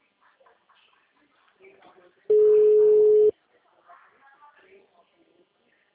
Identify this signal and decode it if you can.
Ringback tone (Europe, 425 Hz), with speech.
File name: out- -2019- - - .523